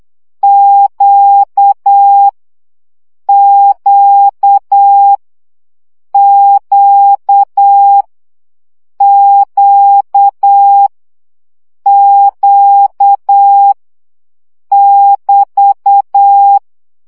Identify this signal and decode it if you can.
Morse code, heard as a keyed tone, about 8 wpm, 800 Hz. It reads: Q Q Q Q Q <BT>
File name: CW gyakorlas q 7 baud